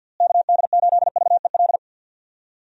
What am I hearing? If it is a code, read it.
Morse code, heard as a keyed tone, about 52 wpm, 700 Hz. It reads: KD8VEL
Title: USA Call Signs - 6 Characters - Repeated 3x - 50wpm